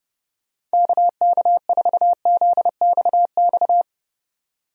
Morse code, heard as a keyed tone, about 31 wpm, 700 Hz. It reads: KK4ZXX
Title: USA Call Signs - 6 Characters - 30wpm